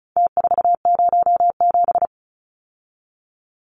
Morse code, heard as a keyed tone, about 34 wpm, 700 Hz. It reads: T407
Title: Numbers - Contest Style - 4-Digit 35wpm